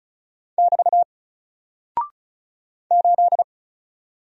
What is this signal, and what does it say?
Morse code, heard as a keyed tone, about 36 wpm, 700 Hz. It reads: <BT> 8
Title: Single Character (Letter:Number:Punctuation:Prosign) 35wpm